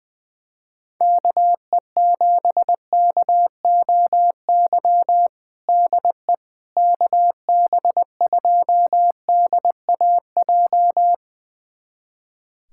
Morse code, heard as a keyed tone, about 20 wpm, 700 Hz. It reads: KE7KOY DE KB2DAJ
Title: USA Callsign DE Callsign - 6 Characters - 20wpm